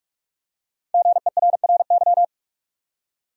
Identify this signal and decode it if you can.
Morse code, heard as a keyed tone, about 46 wpm, 700 Hz. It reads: GERRY
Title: USA Common Names - 45wpm